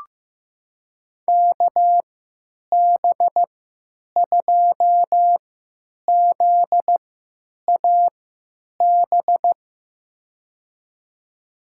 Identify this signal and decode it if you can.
Morse code, heard as a keyed tone, about 15 wpm, spaced as for 9 wpm, 700 Hz. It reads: KB2ZAB
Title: USA Call Signs - 6 Characters - 3x Character Spacing - 15wpm